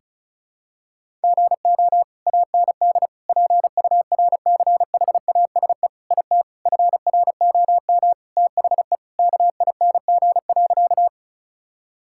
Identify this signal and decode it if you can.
Morse code, heard as a keyed tone, about 36 wpm, 700 Hz. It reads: GO AND PURCHASE IT FROM THE KING.
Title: Sentences from Top 1000 Words - Review Part 5 - 35wpm